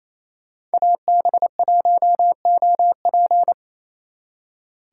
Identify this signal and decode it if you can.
Morse code, heard as a keyed tone, about 33 wpm, 700 Hz. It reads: AB1OP
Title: Speed Racing - USA 2x2 Call Signs 28wpm